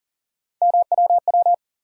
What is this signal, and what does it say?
Morse code, heard as a keyed tone, about 40 wpm, 700 Hz. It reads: MWW
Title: USA Call Sign Postfixes - 3 Letters 40wpm